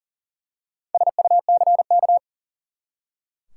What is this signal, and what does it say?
Morse code, heard as a keyed tone, about 40 wpm, 700 Hz. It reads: SUCK
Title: Four Letter Words - 40wpm